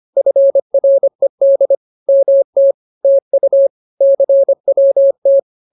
Morse code, heard as a keyed tone, about 25 wpm, 550 Hz. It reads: FRED MT TU CWT
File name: CWT-211-25